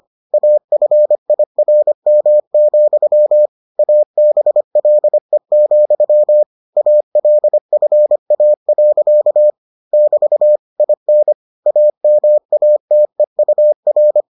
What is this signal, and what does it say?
Morse code, heard as a keyed tone, about 25 wpm, 600 Hz. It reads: AFIRM, ABLE, ALFA. <BT> IN AMATEUR